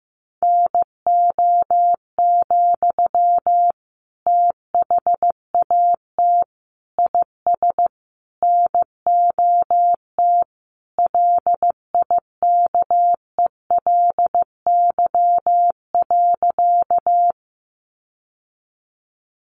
Morse code, heard as a keyed tone, about 15 wpm, 700 Hz. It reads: NO, THAT IS NOT LIKELY.
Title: Sentences from Top 600 Words 15wpm